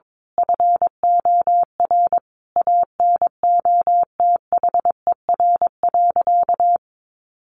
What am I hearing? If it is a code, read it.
Morse code, heard as a keyed tone, about 22 wpm, 700 Hz. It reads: FOR ANOTHER.
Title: Sentences from Top 1400 Words 22wpm